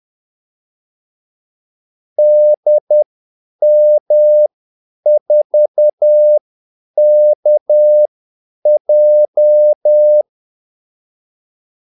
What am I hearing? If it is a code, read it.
Morse code, heard as a keyed tone, about 10 wpm, spaced as for 8 wpm, 600 Hz. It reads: DM4KJ
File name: Callsigns-10-8wpm